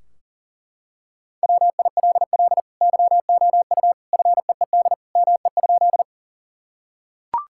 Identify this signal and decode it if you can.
Morse code, heard as a keyed tone, about 40 wpm, 700 Hz. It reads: WILL YOU FEED ME?
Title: Sentences from Top 1600 Words 40wpm